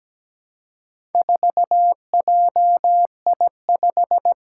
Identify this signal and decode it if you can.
Morse code, heard as a keyed tone, about 17 wpm, 700 Hz. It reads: 4JI5
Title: Four Challenging Letter-Number Combinations 17wpm